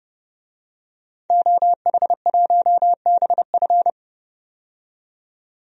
Morse code, heard as a keyed tone, about 30 wpm, 700 Hz. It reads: OH1BF
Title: DXCC-100 Call Signs - 5 Characters - 30wpm